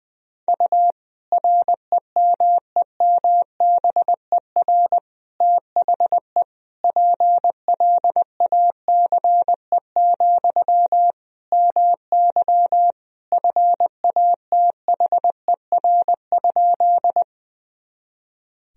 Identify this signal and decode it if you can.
Morse code, heard as a keyed tone, about 20 wpm, 700 Hz. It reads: U REMEMBER THE PLACE, MY FATHER?
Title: Sentences from Top 400 Words and Abbreviations 20wpm